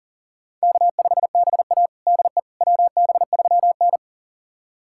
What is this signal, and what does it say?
Morse code, heard as a keyed tone, about 40 wpm, 700 Hz. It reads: K5BA DE WB3N
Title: USA Callsign DE Callsign - 4 Characters - 40wpm